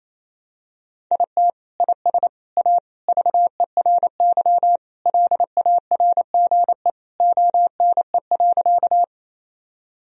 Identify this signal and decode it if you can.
Morse code, heard as a keyed tone, about 28 wpm, 700 Hz. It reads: IT IS A VERY LARGE ONE.
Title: Sentences from Top 1000 Words - Review Part 1 - 28wpm